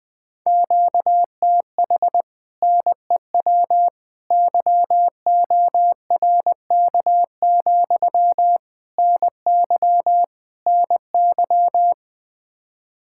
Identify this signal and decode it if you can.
Morse code, heard as a keyed tone, about 20 wpm, 700 Hz. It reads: QTH NEW YORK, NY NY
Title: Speed-Racing Common QSO Phrases - 17wpm